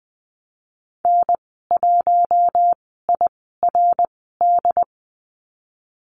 Morse code, heard as a keyed tone, about 20 wpm, spaced as for 15 wpm, 700 Hz. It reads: N1IRD
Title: 128 Lesson 21g - Number 6 - Call Signs 20wpm